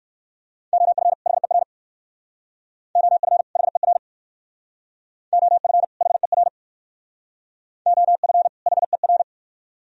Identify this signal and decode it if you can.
Morse code, heard as a keyed tone, about 56 wpm, 690 Hz. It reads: OF HER OF HER OF HER OF HER
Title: Speed-Racing - Sets of 2 Words 45wpm